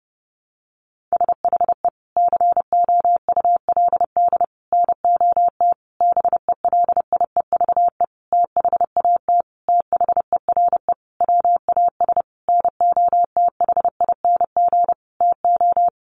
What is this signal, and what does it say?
Morse code, heard as a keyed tone, about 30 wpm, 700 Hz. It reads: SHE COULD NOT BELIEVE THAT THERE WAS NOTHING TO
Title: Sentences from Top 1000 Words 30wpm